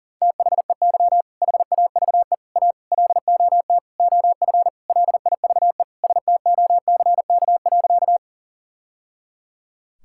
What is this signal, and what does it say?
Morse code, heard as a keyed tone, about 40 wpm, 700 Hz. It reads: THEY HAVE A LOT OF LIVE STOCK.
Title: Sentences from Top 1000 Words - Review Part 5 - 40wpm